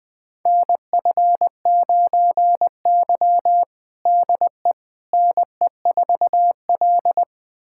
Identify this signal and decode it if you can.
Morse code, heard as a keyed tone, about 20 wpm, 700 Hz. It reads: NF9Y DE NE4L
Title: USA Callsign DE Callsign - 4 Characters - 20wpm